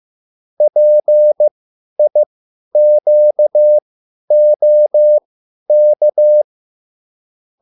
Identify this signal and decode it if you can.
Morse code, heard as a keyed tone, about 15 wpm, spaced as for 10 wpm, 600 Hz. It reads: PIQOK
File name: QSL / CW lesson